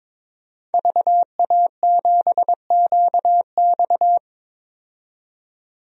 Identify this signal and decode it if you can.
Morse code, heard as a keyed tone, about 22 wpm, 700 Hz. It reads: VA7QX
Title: DXCC-100 Call Signs - 5 Characters - 22wpm